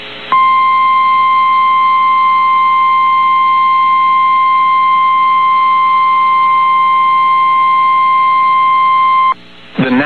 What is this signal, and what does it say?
Emergency Alert System (SAME) tones (the 1050 Hz attention tone). It is heard from 0.3 to 9.4 s.
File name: eas received alerts / L1 19523 audio